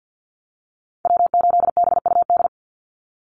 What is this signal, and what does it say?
Morse code, heard as a keyed tone, about 48 wpm, 700 Hz. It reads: W8BUD
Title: Call Signs Repeated 3x - 50wpm